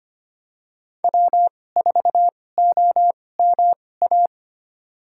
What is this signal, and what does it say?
Morse code, heard as a keyed tone, about 25 wpm, spaced as for 18 wpm, 700 Hz. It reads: W4OMA
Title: USA Call Signs - 5 Characters - 2x Character Spacing - 25wpm